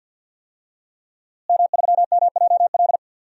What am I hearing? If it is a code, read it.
Morse code, heard as a keyed tone, about 52 wpm, 700 Hz. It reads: M3MJL
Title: DXCC-100 Call Signs - 5 Characters - Repeated 2x - 50wpm